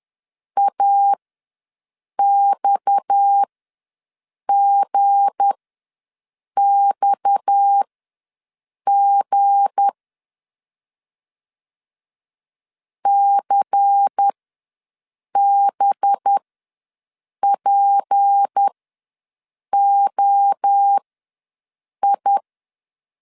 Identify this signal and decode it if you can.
Morse code, heard as a keyed tone, about 10 wpm, spaced as for 6 wpm, 800 Hz. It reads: AXGXG CBPOI
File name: LEZIONI / 7 GF9Z / 7 G ws 10wpm 3843